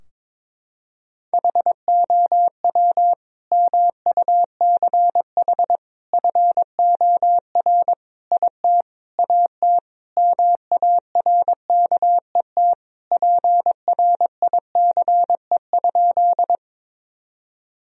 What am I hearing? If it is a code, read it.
Morse code, heard as a keyed tone, about 22 wpm, 700 Hz. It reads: HOW MUCH FOR IT AT MARKET PRICE?
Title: Sentences from Top 1000 Words - Review Part 3 - 22wpm